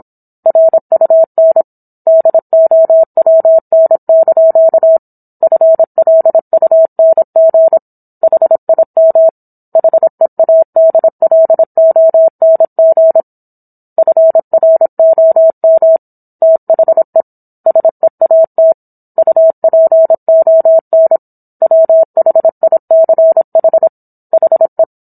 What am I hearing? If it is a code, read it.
Morse code, heard as a keyed tone, about 27 wpm, 650 Hz. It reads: RUN DOWN) FLUNG HIM HEADLONG FROM THE SEAT UPON WHICH HE